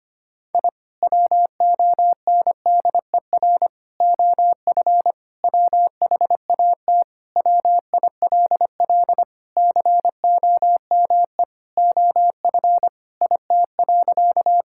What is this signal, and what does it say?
Morse code, heard as a keyed tone, about 25 wpm, 700 Hz. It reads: I WONDER OF WHAT WILL COME OF IT.
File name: practice / 25wpm / Sentences with 2-Word Phrases - 25wpm